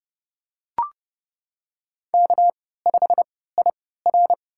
Morse code, heard as a keyed tone, about 30 wpm, spaced as for 17 wpm, 690 Hz. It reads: K5IR
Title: USA Call Signs - 3x Character Spacing - 30wpm